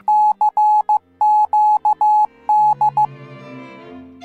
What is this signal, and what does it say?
Morse code, heard as a keyed tone, about 15 wpm, 850 Hz. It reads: CQD